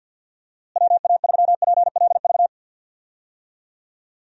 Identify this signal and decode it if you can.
Morse code, heard as a keyed tone, about 49 wpm, 700 Hz. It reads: WA3PLV
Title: USA Call Signs - 6 Characters - Repeated 2x - 50wpm